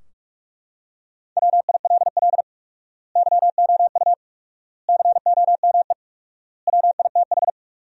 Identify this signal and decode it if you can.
Morse code, heard as a keyed tone, about 41 wpm, 700 Hz. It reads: WILL YOU COME WITH
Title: 4x Word-Spacing - Sentences from Top 200 Words - 45wpm